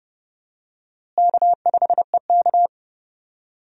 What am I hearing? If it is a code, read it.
Morse code, heard as a keyed tone, about 31 wpm, 700 Hz. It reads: K5EK
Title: USA Call Signs - 4 Characters - Repeated 3x - 30wpm